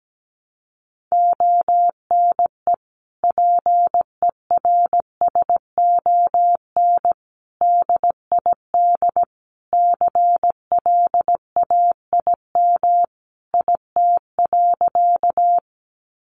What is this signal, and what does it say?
Morse code, heard as a keyed tone, about 17 wpm, 700 Hz. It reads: ONE PERSON DID CLAIM IT.
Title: Sentences from Top 1500 Words 17wpm